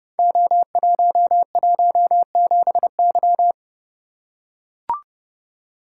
Morse code, heard as a keyed tone, about 30 wpm, 700 Hz. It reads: O117Y
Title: Code Groups - 5 Letter:Numbers - 30wpm